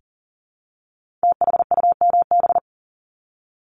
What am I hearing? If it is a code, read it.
Morse code, heard as a keyed tone, about 40 wpm, 700 Hz. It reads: THUMB
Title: Multi-letter Phonemes Repeated 3x 40wpm